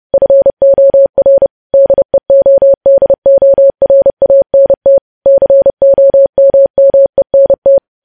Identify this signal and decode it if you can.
Morse code, heard as a keyed tone, about 30 wpm, 550 Hz. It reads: FOR DEODORANT COMMENT
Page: news headlines in morse code at 30 wpm